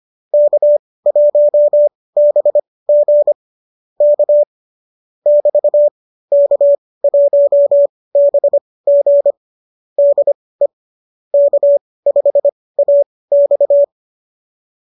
Morse code, heard as a keyed tone, about 25 wpm, spaced as for 18 wpm, 590 Hz. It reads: K1BG K <BT> K1BG DE K5AX